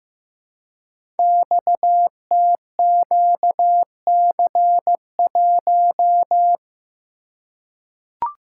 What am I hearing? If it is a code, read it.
Morse code, heard as a keyed tone, about 15 wpm, 700 Hz. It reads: XTQC1
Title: Code Groups - 5 Letter:Numbers - 15wpm